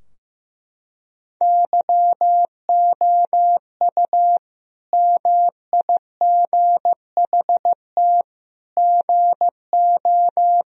Morse code, heard as a keyed tone, about 15 wpm, 700 Hz. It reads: YOU MIGHT GO
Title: Sentences from Top 1000 Words - Review Part 4 - 15wpm